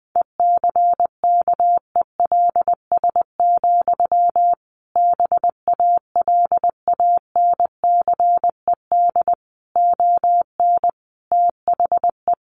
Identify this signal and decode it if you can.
Morse code, heard as a keyed tone, about 20 wpm, 700 Hz. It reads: ECKELS, BALANCED ON THE
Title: A Sound Of Thunder 20wpm